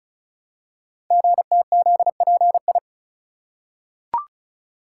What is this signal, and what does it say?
Morse code, heard as a keyed tone, about 36 wpm, 700 Hz. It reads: GTZPI
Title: Code Groups - 5 Letters - 35wpm